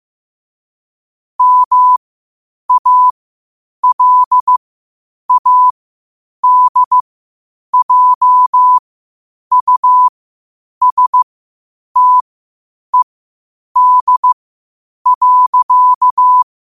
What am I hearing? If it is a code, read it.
Morse code, heard as a keyed tone, about 15 wpm, spaced as for 8 wpm, 1000 Hz. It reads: MALADJUSTED.
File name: qotdfiles / 8wpm